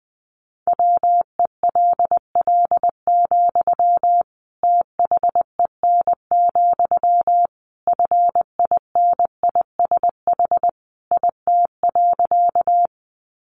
Morse code, heard as a keyed tone, about 20 wpm, 700 Hz. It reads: WELL, THEN, FINISH IT.
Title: Sentences from Top 1200 Words 20wpm